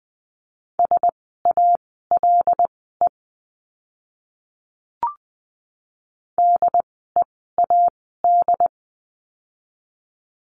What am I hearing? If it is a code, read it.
Morse code, heard as a keyed tone, about 20 wpm, spaced as for 14 wpm, 700 Hz. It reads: SALE DEAD
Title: 075 Lesson 13f - Letter L - Words 20wpm